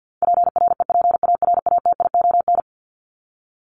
Morse code, heard as a keyed tone, about 50 wpm, 700 Hz. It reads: PREPARATION
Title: Top 1251-1500 Words 50wpm